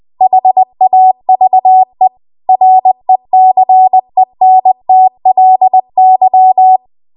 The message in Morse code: HAVE RECENTLY